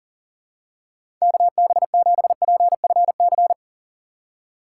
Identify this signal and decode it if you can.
Morse code, heard as a keyed tone, about 40 wpm, 700 Hz. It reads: KB7PFC
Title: USA Call Signs - 6 Characters - 3x Character Spacing - 40wpm